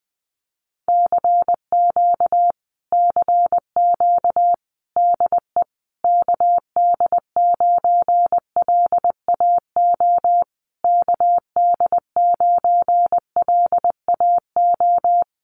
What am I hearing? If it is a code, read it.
Morse code, heard as a keyed tone, about 20 wpm, 700 Hz. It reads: CQ CQ DE KD9LAO KD9LAO
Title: Common QSO Phrases Encore 20wpm